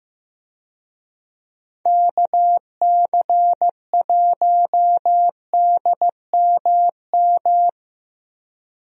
Morse code, heard as a keyed tone, about 15 wpm, 700 Hz. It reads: KC1DMM
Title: USA Call Signs - 6 Characters - 15wpm